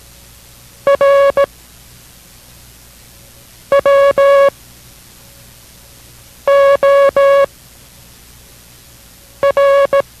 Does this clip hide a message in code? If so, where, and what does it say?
Yes: Morse code from 0.9 to 10.0 s, heard as a keyed tone, about 18 wpm, spaced as for 4 wpm, 560 Hz. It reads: R WOR